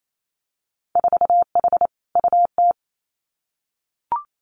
Morse code, heard as a keyed tone, about 28 wpm, 700 Hz. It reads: 4H UT
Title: Winter Field Day Exchanges Encore 28wpm